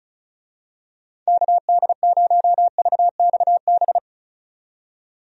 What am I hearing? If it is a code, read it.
Morse code, heard as a keyed tone, about 36 wpm, 700 Hz. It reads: KD0VXB
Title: USA Call Signs - 6 Characters - 35wpm